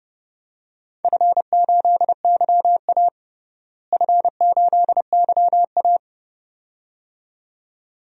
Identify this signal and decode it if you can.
Morse code, heard as a keyed tone, about 30 wpm, 700 Hz. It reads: F8YA F8YA
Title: DXCC-100 Call Signs - 4 Characters - Repeated 2x - 30wpm